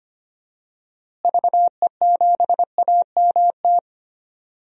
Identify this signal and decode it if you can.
Morse code, heard as a keyed tone, about 25 wpm, 700 Hz. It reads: VE7AMT
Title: DXCC-100 Call Signs - 6 Characters - Repeated 2x - 25wpm